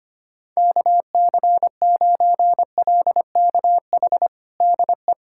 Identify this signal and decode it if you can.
Morse code, heard as a keyed tone, about 25 wpm, 700 Hz. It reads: KC9LKH DE
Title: USA Callsign DE Callsign 25wpm